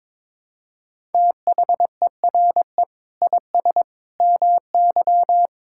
Morse code, heard as a keyed tone, about 22 wpm, 700 Hz. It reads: THERE IS MY
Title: Sentences from Top 1000 Words - Review Part 6 - 22wpm